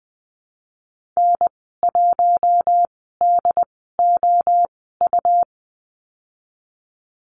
Morse code, heard as a keyed tone, about 20 wpm, spaced as for 14 wpm, 700 Hz. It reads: N1DOU
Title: 161 Lesson 26g - Letter g - Call Signs 20wpm